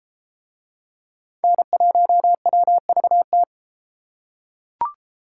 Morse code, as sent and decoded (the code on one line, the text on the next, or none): -. .---- .-- ...- -
N1WVT